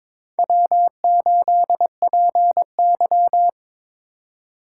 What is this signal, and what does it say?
Morse code, heard as a keyed tone, about 26 wpm, 700 Hz. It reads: W8PY
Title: Speed Racing - USA 1x2 Call Signs 22wpm